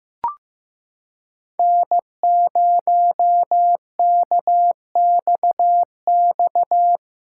Morse code, heard as a keyed tone, about 15 wpm, 700 Hz. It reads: N0KXX
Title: USA Callsign DE Callsign - 5 Characters - 15wpm